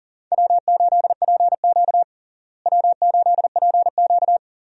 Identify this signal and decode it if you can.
Morse code, heard as a keyed tone, about 40 wpm, 700 Hz. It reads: W8PQ W8PQ
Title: USA Call Signs - 4 Characters - Repeated 2x - 40wpm